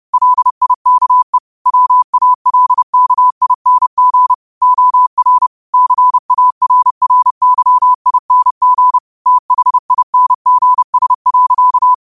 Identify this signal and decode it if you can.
Morse code, heard as a keyed tone, about 30 wpm, 1000 Hz. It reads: LIKE WALKING OR CARRYING THINGS.
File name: qotdfiles / 30wpm